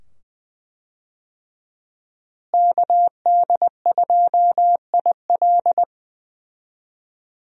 Morse code, heard as a keyed tone, about 20 wpm, 700 Hz. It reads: KD2IL
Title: USA Call Signs - 5 Characters - Repeated 2x - 20wpm